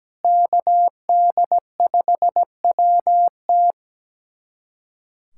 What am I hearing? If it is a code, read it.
Morse code, heard as a keyed tone, about 17 wpm, 700 Hz. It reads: KD5WT
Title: USA Callsign DE Callsign 17wpm